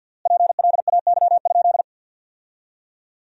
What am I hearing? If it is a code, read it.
Morse code, heard as a keyed tone, about 50 wpm, 700 Hz. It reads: PLAY?
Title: 3x Word-Spacing - Sentences from Top 200 Words - 50wpm